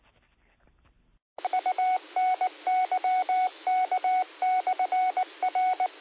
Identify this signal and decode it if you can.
Morse code, heard as a keyed tone, about 19 wpm, 740 Hz. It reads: UNYK/R
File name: qsorec SimplexLogic